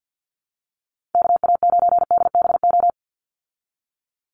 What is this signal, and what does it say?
Morse code, heard as a keyed tone, about 50 wpm, 700 Hz. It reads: KA9DBO
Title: USA Call Signs 50wpm